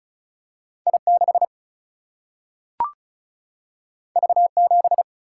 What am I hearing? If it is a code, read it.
Morse code, heard as a keyed tone, about 36 wpm, 700 Hz. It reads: I 6 V 7
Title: Single-Letter Single-Number 35wpm